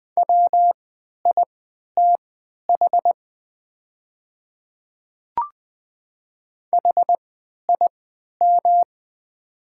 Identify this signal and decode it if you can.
Morse code, heard as a keyed tone, about 20 wpm, spaced as for 11 wpm, 690 Hz. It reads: WITH HIM
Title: 3x Character Spacing - Top 300 Words - 20wpm